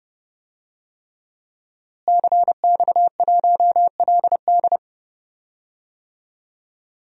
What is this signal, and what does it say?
Morse code, heard as a keyed tone, about 30 wpm, 700 Hz. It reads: CX1LD
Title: DXCC-100 Call Signs - 5 Characters - Repeated 3x - 30wpm